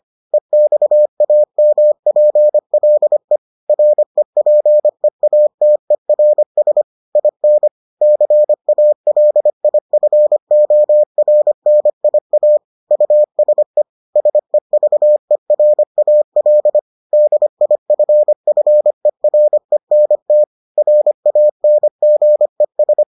Morse code, heard as a keyed tone, about 25 wpm, 600 Hz. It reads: EXAMPLE REPEATERS IN CALIFORNIA USE SEVERAL DIFFERENT RANGES